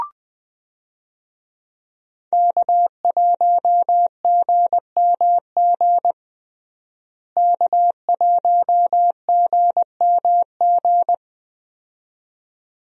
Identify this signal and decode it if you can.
Morse code, heard as a keyed tone, about 20 wpm, 700 Hz. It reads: K1GMG K1GMG
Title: USA Call Signs - 5 Characters - Repeated 2x - 20wpm